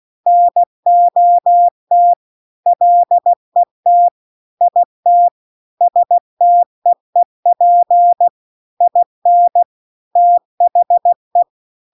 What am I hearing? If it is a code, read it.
Morse code, heard as a keyed tone, about 16 wpm, 700 Hz. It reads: NOT LET IT STEEP IN THE